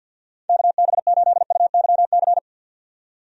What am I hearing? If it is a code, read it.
Morse code, heard as a keyed tone, about 53 wpm, 700 Hz. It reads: KB8UYC
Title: USA Call Signs - 6 Characters - 50wpm